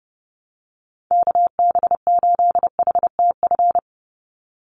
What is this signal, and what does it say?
Morse code, heard as a keyed tone, about 28 wpm, 700 Hz. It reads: KB8HTF
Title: USA Call Signs 30wpm